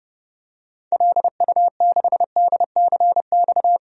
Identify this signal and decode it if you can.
Morse code, heard as a keyed tone, about 32 wpm, 700 Hz. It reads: LU6DCX
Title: DXCC-100 Call Signs - 6 Characters - 30wpm